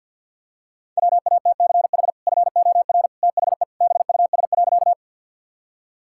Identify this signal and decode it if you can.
Morse code, heard as a keyed tone, about 50 wpm, 700 Hz. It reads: WATCH FOR THE BUS.
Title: Sentences from Top 1000 Words - Review Part 6 - 50wpm